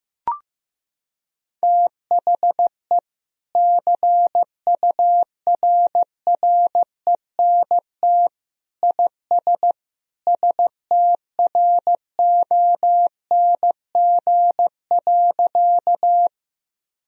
Morse code, heard as a keyed tone, about 16 wpm, 700 Hz. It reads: THE CURRENT IS STRONG.
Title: Sentences from Top 1000 Words - Review Part 4 - 15wpm